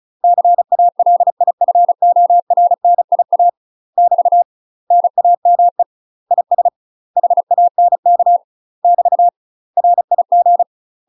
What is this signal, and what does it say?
Morse code, heard as a keyed tone, about 34 wpm, 700 Hz. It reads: CALIFORNIA <BT> NAME IS HANK <BT> RIG
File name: UK MASTER QSOs TRAINING / 35WPM / UK MASTER QSOs 0003